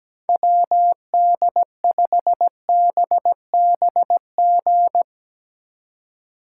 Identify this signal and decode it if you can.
Morse code, heard as a keyed tone, about 18 wpm, 700 Hz. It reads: WD5BBG
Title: Speed Racing - USA 2x3 Call Signs 17wpm